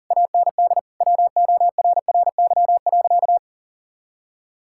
Morse code, heard as a keyed tone, about 40 wpm, 700 Hz. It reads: AND WORRY.
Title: Pangram Challenge v2 - 40wpm